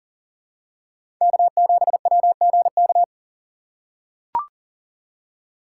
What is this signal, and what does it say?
Morse code, heard as a keyed tone, about 40 wpm, 700 Hz. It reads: K7WGK
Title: USA Call Signs - 5 Characters - 6x Character Spacing - 40wpm